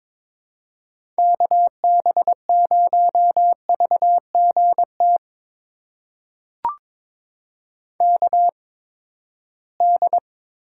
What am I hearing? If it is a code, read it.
Morse code, heard as a keyed tone, about 22 wpm, spaced as for 15 wpm, 700 Hz. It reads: KB0VGT KD
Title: USA Call Signs - 6 Characters - 8x Character Spacing - 22wpm